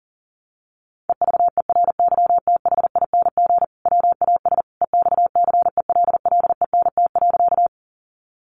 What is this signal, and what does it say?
Morse code, heard as a keyed tone, about 37 wpm, 700 Hz. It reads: EVERYTHING WAS EXCELLENT.
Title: Sentences from Top 1000 Words 40wpm